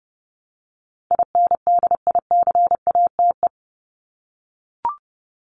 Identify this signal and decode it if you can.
Morse code, heard as a keyed tone, about 30 wpm, 700 Hz. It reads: INDICATE
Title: ED Suffix - Top 100 Words with Base Words 30wpm